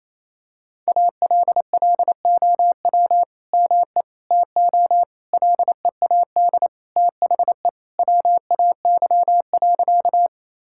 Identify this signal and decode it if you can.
Morse code, heard as a keyed tone, about 28 wpm, 700 Hz. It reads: ALLOW ME TO LEAD THE WAY.
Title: Sentences from Top 1000 Words - Review Part 5 - 28wpm